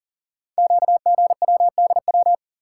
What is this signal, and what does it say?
Morse code, heard as a keyed tone, about 38 wpm, 700 Hz. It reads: QGWDW
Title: Code Groups - 5 Letters - 40wpm